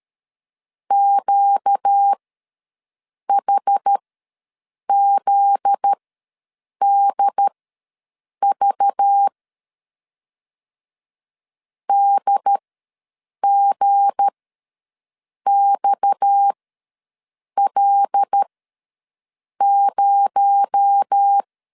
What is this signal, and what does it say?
Morse code, heard as a keyed tone, about 13 wpm, spaced as for 6 wpm, 800 Hz. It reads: QHZDV DGXL0
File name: LEZIONI / 8 YJ45 / 8 Y ws 12wpm 3409